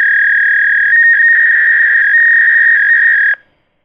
Fax machine handshake.